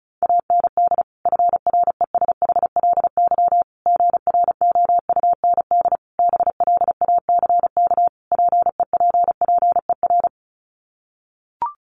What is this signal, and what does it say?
Morse code, heard as a keyed tone, about 34 wpm, 700 Hz. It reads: AND FRESHLY GROUND BLACK PEPPER
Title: Sets of 5 Words Encore 35wpm